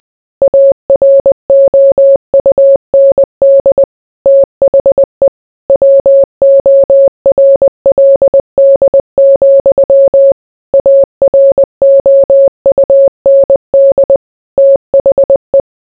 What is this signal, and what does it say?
Morse code, heard as a keyed tone, about 20 wpm, 550 Hz. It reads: AROUND THE WORLD, AROUND THE